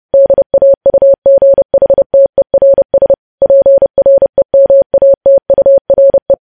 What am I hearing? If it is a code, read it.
Morse code, heard as a keyed tone, about 30 wpm, 550 Hz. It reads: DAUGHTERS PREMATURE
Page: news headlines in morse code at 30 wpm